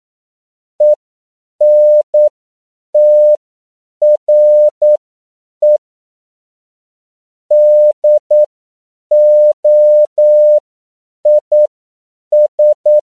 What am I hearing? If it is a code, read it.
Morse code, heard as a keyed tone, about 9 wpm, spaced as for 7 wpm, 600 Hz. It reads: ENTRE DOIS
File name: CW / 10PPM